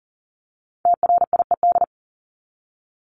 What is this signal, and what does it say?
Morse code, heard as a keyed tone, about 40 wpm, 700 Hz. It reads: TRIED
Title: ED Suffix - Top 100 Words with Base Words 40wpm